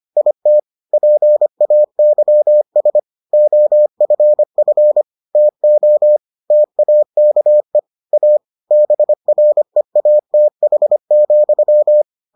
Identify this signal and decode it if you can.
Morse code, heard as a keyed tone, about 25 wpm, 600 Hz. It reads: IT PAYS OFF TO TAKE A BREATH,